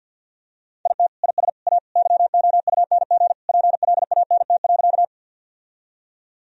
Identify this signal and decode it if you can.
Morse code, heard as a keyed tone, about 52 wpm, 700 Hz. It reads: IT IS A YOUNG PLANT.
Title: Sentences from Top 1000 Words - Review Part 1 - 50wpm